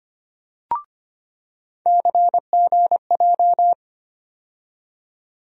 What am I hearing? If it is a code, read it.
Morse code, heard as a keyed tone, about 25 wpm, 700 Hz. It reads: CGJ
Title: USA Call Sign Postfixes - 3 Letters 25wpm